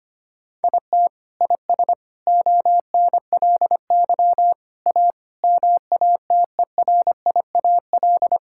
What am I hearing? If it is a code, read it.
Morse code, heard as a keyed tone, about 25 wpm, 700 Hz. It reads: IT IS ONLY A MATERIAL
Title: Sentences from Top 1000 Words - Review Part 5 - 25wpm